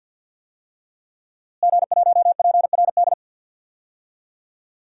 Morse code, heard as a keyed tone, about 47 wpm, 700 Hz. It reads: G1PRD
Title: DXCC-100 Call Signs - 5 Characters - Repeated 2x - 50wpm